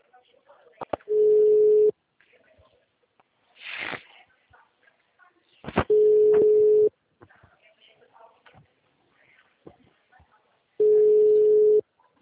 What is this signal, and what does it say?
Ringback tone (Europe, 425 Hz), with speech.
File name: out- -2003- - - .3262